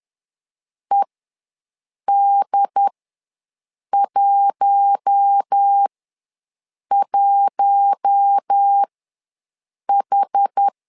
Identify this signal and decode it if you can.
Morse code, heard as a keyed tone, about 10 wpm, spaced as for 6 wpm, 800 Hz. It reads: ED11H
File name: LEZIONI / 4 Q1 ws 10wpm 2538